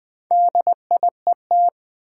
Morse code, heard as a keyed tone, about 20 wpm, 700 Hz. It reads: DIET
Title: Build A Word - 2 Letters At A Time - 4-Letter Words - 20wpm